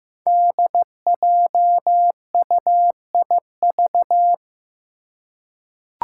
Morse code, heard as a keyed tone, about 15 wpm, 700 Hz. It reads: DJUIV
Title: Five Challenging Letter-Number Combinations 15wpm